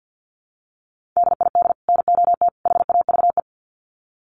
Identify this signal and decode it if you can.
Morse code, heard as a keyed tone, about 49 wpm, 700 Hz. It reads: DID NOT HAVE
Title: Sets of 3 Words 50wpm